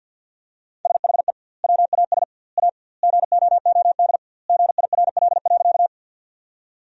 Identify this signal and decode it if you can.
Morse code, heard as a keyed tone, about 50 wpm, 700 Hz. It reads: SHE WAS A GOOD GIRL.
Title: 2x Word-Spacing - Sentences from Top 400 Words - 50wpm